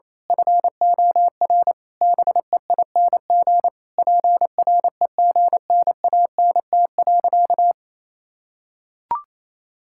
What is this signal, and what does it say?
Morse code, heard as a keyed tone, about 28 wpm, 700 Hz. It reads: FOR BEING PREGNANT.
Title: Sentences from Top 1000 Words - Review Part 6 - 28wpm